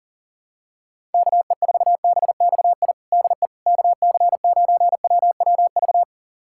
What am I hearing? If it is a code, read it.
Morse code, heard as a keyed tone, about 40 wpm, 700 Hz. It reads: KE4BXI DE KC9WWV
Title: USA Callsign DE Callsign 40wpm